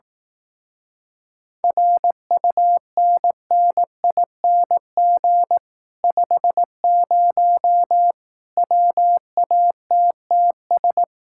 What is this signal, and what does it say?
Morse code, heard as a keyed tone, about 18 wpm, 700 Hz. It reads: RUNNING 50 WATTS
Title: Speed-Racing Common QSO Phrases Encore - 15wpm